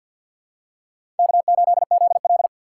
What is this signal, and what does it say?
Morse code, heard as a keyed tone, about 53 wpm, 700 Hz. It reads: K8ZL
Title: USA Call Signs - 3x Character Spacing - 50wpm